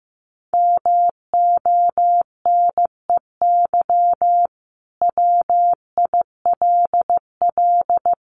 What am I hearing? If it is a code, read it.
Morse code, heard as a keyed tone, about 15 wpm, 700 Hz. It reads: MONEY WILL
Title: Sentences from Top 1500 Words 15wpm